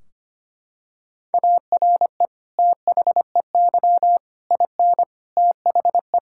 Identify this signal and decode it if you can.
Morse code, heard as a keyed tone, about 25 wpm, 700 Hz. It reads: ARE THEY IN THE
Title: Sentences from Top 1600 Words 25wpm